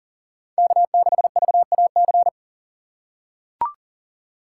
Morse code, heard as a keyed tone, about 38 wpm, 700 Hz. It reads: K6VAC
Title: USA Call Signs - 5 Characters - 4x Character Spacing - 40wpm